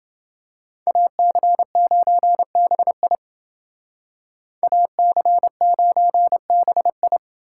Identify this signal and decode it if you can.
Morse code, heard as a keyed tone, about 28 wpm, 700 Hz. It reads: AC9BI AC9BI